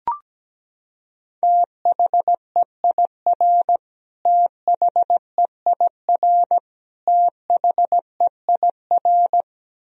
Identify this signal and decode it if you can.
Morse code, heard as a keyed tone, about 17 wpm, 700 Hz. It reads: THEIR THEIR THEIR
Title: Top 100 Words Repeated 17wpm